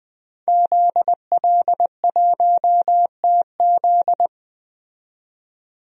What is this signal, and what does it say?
Morse code, heard as a keyed tone, about 20 wpm, 700 Hz. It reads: ZL1TZ
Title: DXCC-100 Call Signs - 5 Characters - 20wpm